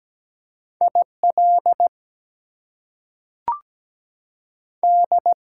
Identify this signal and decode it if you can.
Morse code, heard as a keyed tone, about 17 wpm, 700 Hz. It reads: I L D
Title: US States and Canadian Provinces 17wpm